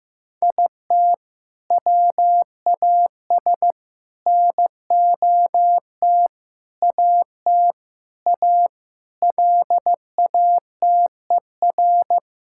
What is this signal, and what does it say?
Morse code, heard as a keyed tone, about 15 wpm, 700 Hz. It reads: IT WAS NOT AT A LATER
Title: Sentences from Top 1000 Words - Review Part 5 - 15wpm